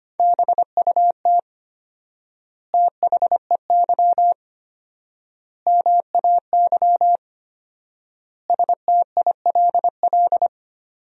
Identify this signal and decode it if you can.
Morse code, heard as a keyed tone, about 25 wpm, 700 Hz. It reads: BUT THEY MAY STILL
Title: 4x Word-Spacing - Sentences from Top 300 Words - 25wpm